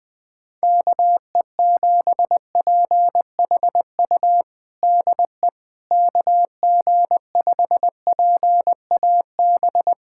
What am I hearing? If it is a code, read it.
Morse code, heard as a keyed tone, about 21 wpm, 700 Hz. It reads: KE7PHU DE KG5PAB
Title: USA Callsign DE Callsign - 6 Characters - 20wpm